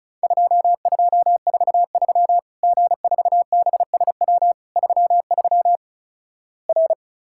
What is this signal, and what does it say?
Morse code, heard as a keyed tone, about 36 wpm, 700 Hz. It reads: 2243 G4BSW 33 R
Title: Worked All Europe DX Contest - QTC Exchange - 35wpm